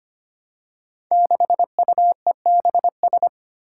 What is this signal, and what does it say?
Morse code, heard as a keyed tone, about 25 wpm, 700 Hz. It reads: 6UEBS
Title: Five Challenging Letter-Number Combinations 25wpm